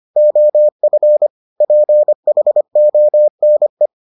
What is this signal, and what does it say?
Morse code, heard as a keyed tone, about 25 wpm, 600 Hz. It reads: OF PHONE